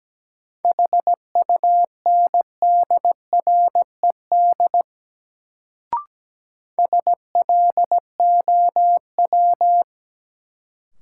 Morse code, heard as a keyed tone, about 17 wpm, 700 Hz. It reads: HUNDRED SLOW
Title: Sentences from Top 400 Words - New Word Drill - 17wpm